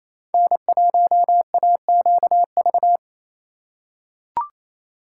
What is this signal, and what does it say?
Morse code, heard as a keyed tone, about 28 wpm, 700 Hz. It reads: N1AQV
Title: USA Call Signs - 5 Characters - 4x Character Spacing - 28wpm